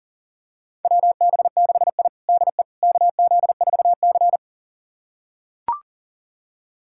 Morse code, heard as a keyed tone, about 40 wpm, 700 Hz. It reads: WB6I DE KZ4C
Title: USA Callsign DE Callsign - 4 Characters - 40wpm